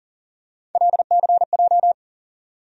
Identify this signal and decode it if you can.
Morse code, heard as a keyed tone, about 40 wpm, 700 Hz. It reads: LCJ
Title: USA Call Sign Postfixes - 3 Letters 40wpm